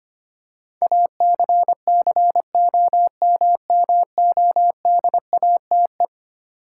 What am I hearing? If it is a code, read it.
Morse code, heard as a keyed tone, about 25 wpm, 700 Hz. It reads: ACCOMMODATE
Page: morse-code-ninja-practice-25wpm